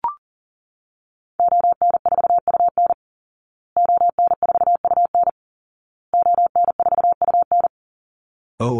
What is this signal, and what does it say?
Morse code, heard as a keyed tone, about 40 wpm, 700 Hz. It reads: ON4UN ON4UN ON4UN
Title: Call Signs Repeated 2x - 40wpm